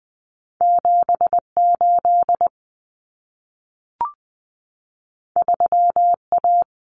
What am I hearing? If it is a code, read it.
Morse code, heard as a keyed tone, about 20 wpm, 700 Hz. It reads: 78 3A
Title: Common Cut Numbers - 2-Digit 20wpm